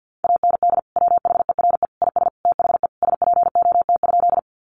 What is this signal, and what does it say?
Morse code, heard as a keyed tone, about 50 wpm, 700 Hz. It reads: AND WHERE IS THE SPOT?
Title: Sentences from Top 1100 Words 50wpm